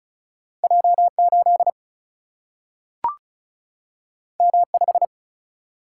Morse code, heard as a keyed tone, about 36 wpm, 700 Hz. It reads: J8 M5
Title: Single-Letter Single-Number 35wpm